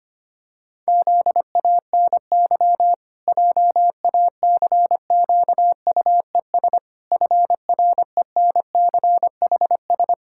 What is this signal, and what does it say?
Morse code, heard as a keyed tone, about 25 wpm, 700 Hz. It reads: ZANY JACQUES FRENCHS
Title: Pangram Challenge v2 - 25wpm